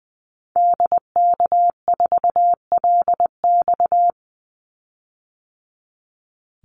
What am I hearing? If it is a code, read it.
Morse code, heard as a keyed tone, about 20 wpm, 700 Hz. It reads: DK4LX
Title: Call Signs - 20wpm